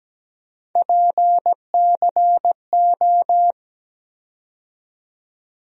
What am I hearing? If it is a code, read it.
Morse code, heard as a keyed tone, about 17 wpm, 700 Hz. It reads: PCO
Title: USA Call Sign Postfixes - 3 Letters 17wpm